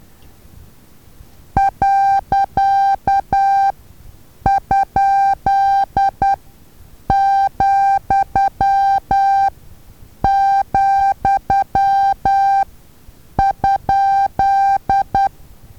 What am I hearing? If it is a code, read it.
Morse code, heard as a keyed tone, about 10 wpm, spaced as for 7 wpm, 790 Hz. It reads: .?,,?
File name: Taviro HA6VA A